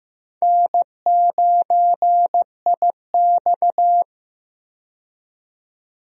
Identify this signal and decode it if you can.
Morse code, heard as a keyed tone, about 15 wpm, 700 Hz. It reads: N9IX
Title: USA Call Signs - 4 Characters - 15wpm